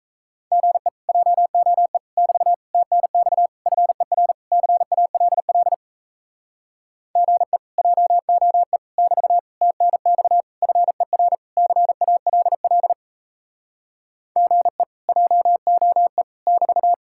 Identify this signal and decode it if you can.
Morse code, heard as a keyed tone, about 37 wpm, 700 Hz. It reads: GE JOE <BT> TNX FER CALL GE JOE <BT> TNX FER CALL GE JOE <BT>